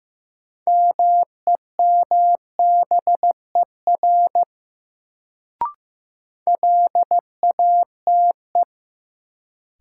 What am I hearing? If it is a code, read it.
Morse code, heard as a keyed tone, about 15 wpm, 700 Hz. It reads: MEMBER LATE
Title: Sets of 3 Words - New Word Drill - 15wpm